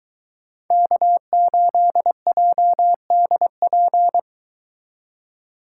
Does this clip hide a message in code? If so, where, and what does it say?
Yes: Morse code from 0.7 to 4.2 s, heard as a keyed tone, about 23 wpm, 700 Hz. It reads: K8JDP